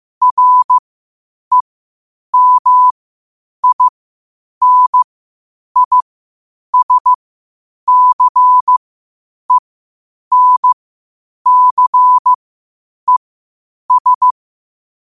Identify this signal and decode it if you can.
Morse code, heard as a keyed tone, about 15 wpm, spaced as for 8 wpm, 1000 Hz. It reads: REMINISCENCES